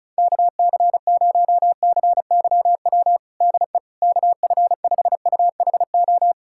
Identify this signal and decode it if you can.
Morse code, heard as a keyed tone, about 36 wpm, 700 Hz. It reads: KC0CYW DE KF5UHO
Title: USA Callsign DE Callsign - 6 Characters - 35wpm